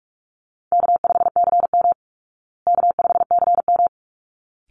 Morse code, heard as a keyed tone, about 44 wpm, 700 Hz. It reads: K5CM K5CM
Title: Call Signs Repeated 3x - 45wpm